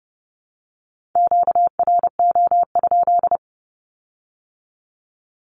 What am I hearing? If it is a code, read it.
Morse code, heard as a keyed tone, about 31 wpm, 700 Hz. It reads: QRO?
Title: Q-Codes with Spoken Meaning - 30wpm